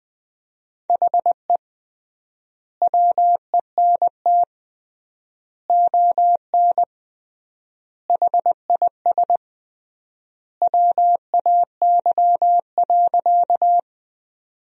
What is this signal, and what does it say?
Morse code, heard as a keyed tone, about 20 wpm, 700 Hz. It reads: HE WENT ON HIS WAY.
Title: 3x Word-Spacing - Sentences from Top 200 Words - 20wpm